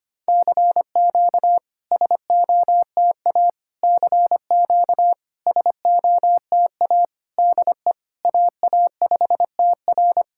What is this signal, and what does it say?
Morse code, heard as a keyed tone, about 25 wpm, 700 Hz. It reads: CQ SOTA CQ SOTA DE AA5TR
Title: SOTA QSO Practice 25wpm